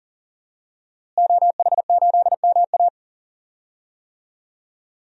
Morse code, heard as a keyed tone, about 40 wpm, 700 Hz. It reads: OH8MA
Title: DXCC-100 Call Signs - 5 Characters - Repeated 3x - 40wpm